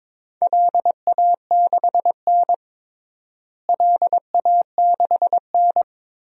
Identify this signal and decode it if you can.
Morse code, heard as a keyed tone, about 22 wpm, 700 Hz. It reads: LA6N LA6N
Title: DXCC-100 Call Signs - 4 Characters - Repeated 3x - 22wpm